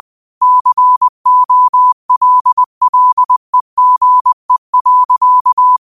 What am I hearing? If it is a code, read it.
Morse code, heard as a keyed tone, about 20 wpm, 1000 Hz. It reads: COLLEGE.
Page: Quotes for Sat, 16 Aug 2025 in Morse Code at 20 words per minute.